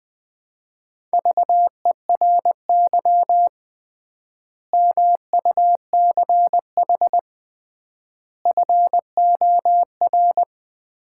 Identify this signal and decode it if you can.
Morse code, heard as a keyed tone, about 20 wpm, 700 Hz. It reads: VERY MUCH FOR
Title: 3x Word-Spacing - Sets of 5 Words - 20wpm